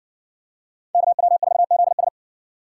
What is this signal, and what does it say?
Morse code, heard as a keyed tone, about 60 wpm, 700 Hz. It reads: KW4ZS